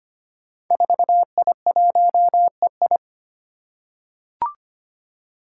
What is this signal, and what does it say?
Morse code, heard as a keyed tone, about 25 wpm, 700 Hz. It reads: 4I1EI
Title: Five Challenging Letter-Number Combinations 25wpm